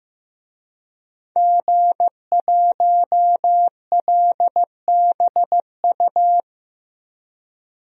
Morse code, heard as a keyed tone, about 15 wpm, 700 Hz. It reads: G1LBU
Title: DXCC-100 Call Signs - 5 Characters - Repeated 2x - 15wpm